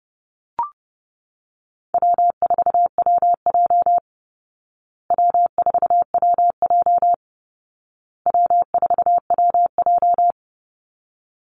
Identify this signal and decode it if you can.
Morse code, heard as a keyed tone, about 28 wpm, 700 Hz. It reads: W4WJ W4WJ W4WJ
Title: Easy Call Signs Repeated 3x - 30wpm